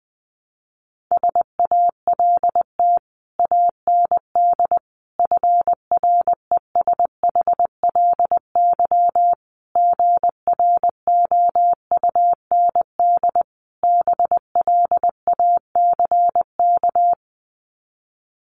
Morse code, heard as a keyed tone, about 20 wpm, 700 Hz. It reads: SALT AND FRESHLY GROUND BLACK
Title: Sets of 5 Words Encore 20wpm